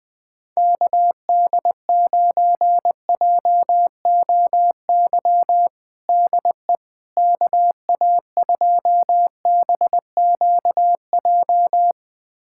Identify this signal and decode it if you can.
Morse code, heard as a keyed tone, about 20 wpm, 700 Hz. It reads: KD9JOY DE KA2BQJ
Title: USA Callsign DE Callsign - 6 Characters - 20wpm